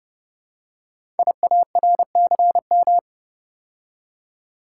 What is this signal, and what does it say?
Morse code, heard as a keyed tone, about 30 wpm, 700 Hz. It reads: IARCM
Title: Code Groups - 5 Letters - 30wpm